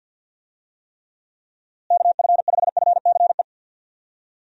Morse code, heard as a keyed tone, about 51 wpm, 700 Hz. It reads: KF5FCE
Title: USA Call Signs - 6 Characters - Repeated 2x - 50wpm